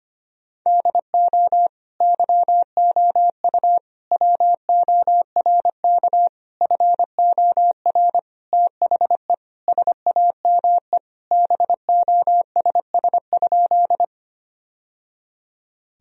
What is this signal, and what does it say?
Morse code, heard as a keyed tone, about 25 wpm, 700 Hz. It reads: DO YOU WORK FOR THE SAME BOSS?
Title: Sentences from Top 1000 Words - Review Part 4 - 25wpm